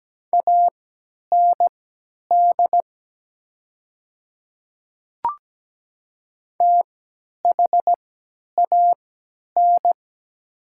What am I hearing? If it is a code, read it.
Morse code, heard as a keyed tone, about 17 wpm, spaced as for 10 wpm, 700 Hz. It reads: AND THAN
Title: 3x Character Spacing - Top 200 Words - 17wpm